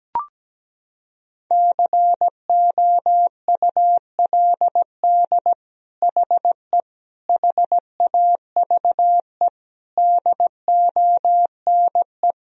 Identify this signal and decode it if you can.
Morse code, heard as a keyed tone, about 17 wpm, 700 Hz. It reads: COULD HE HAVE DONE
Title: Sentences from Top 1000 Words - Review Part 3 - 17wpm